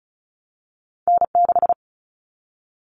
Morse code, heard as a keyed tone, about 36 wpm, 700 Hz. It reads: N6
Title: All Cut Numbers - 2-Digit 35wpm